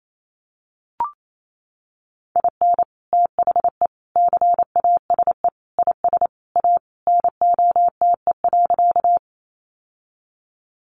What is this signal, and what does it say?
Morse code, heard as a keyed tone, about 28 wpm, 700 Hz. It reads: IN THE CASE IS A NOTE.
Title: Sentences from Top 700 Words 28wpm